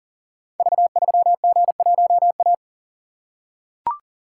Morse code, heard as a keyed tone, about 39 wpm, 690 Hz. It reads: V3G1A
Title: Code Groups - 5 Letter:Numbers - 40wpm